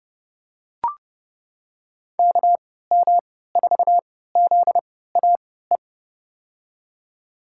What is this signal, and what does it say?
Morse code, heard as a keyed tone, about 30 wpm, spaced as for 17 wpm, 700 Hz. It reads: KM4ZAE
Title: USA Call Signs - 3x Character Spacing - 30wpm